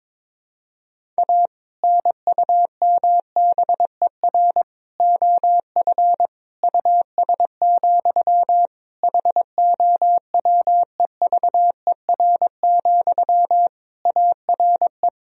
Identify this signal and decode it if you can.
Morse code, heard as a keyed tone, about 22 wpm, 700 Hz. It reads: A NUMBER OF US, HOWEVER, ARE
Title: Sentences from Top 1000 Words - Review Part 1 - 22wpm